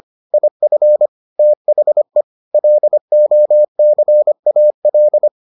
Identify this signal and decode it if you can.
Morse code, heard as a keyed tone, about 25 wpm, 600 Hz. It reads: IF THE LOCAL